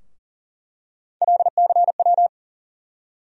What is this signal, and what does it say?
Morse code, heard as a keyed tone, about 40 wpm, 700 Hz. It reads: LCW
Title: USA Call Sign Postfixes - 3 Letters 40wpm